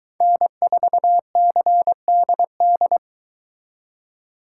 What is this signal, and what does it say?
Morse code, heard as a keyed tone, about 22 wpm, 700 Hz. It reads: N4CDD